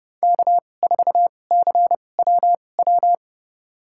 Morse code, heard as a keyed tone, about 30 wpm, spaced as for 22 wpm, 700 Hz. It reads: K4CWW
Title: USA Call Signs - 5 Characters - 2x Character Spacing - 30wpm